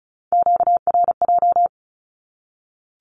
Morse code, heard as a keyed tone, about 34 wpm, 700 Hz. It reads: QRJ
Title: Q-Codes Repeated 3x 35wpm